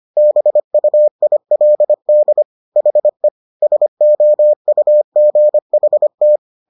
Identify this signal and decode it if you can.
Morse code, heard as a keyed tone, about 25 wpm, 600 Hz. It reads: BUILD HE SOUGHT